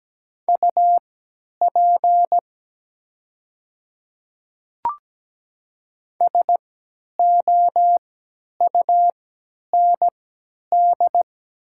Morse code, heard as a keyed tone, about 17 wpm, spaced as for 10 wpm, 700 Hz. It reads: UP SOUND
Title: 3x Character Spacing - Top 100 Words - 17wpm